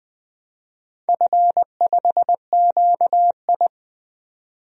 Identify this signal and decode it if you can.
Morse code, heard as a keyed tone, about 20 wpm, 700 Hz. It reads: F5QI
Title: DXCC-100 Call Signs - 4 Characters - 20wpm